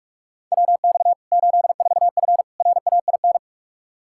Morse code, heard as a keyed tone, about 46 wpm, 700 Hz. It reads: WX 84F RAIN
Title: K1USN SST QSO Practice 45wpm